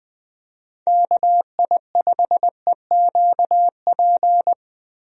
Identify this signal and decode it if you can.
Morse code, heard as a keyed tone, about 21 wpm, 700 Hz. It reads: KI5EQP
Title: USA Call Signs - 6 Characters - Repeated 3x - 20wpm